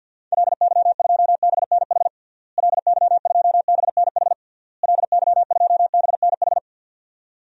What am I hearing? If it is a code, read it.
Morse code, heard as a keyed tone, about 47 wpm, 700 Hz. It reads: LY2BNH LY2BNH LY2BNH
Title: DXCC-100 Call Signs - 6 Characters - Repeated 3x - 50wpm